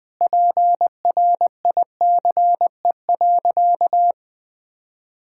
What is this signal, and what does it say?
Morse code, heard as a keyed tone, about 20 wpm, 700 Hz. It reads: PRICE.
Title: Sentences from Top 1000 Words - Review Part 6 - 20wpm